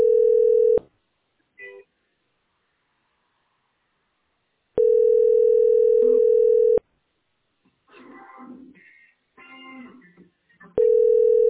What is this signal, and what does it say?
Ringback tone (North America), 440 + 480 Hz.
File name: 5545cd2d-d7b3-4d4c-8e85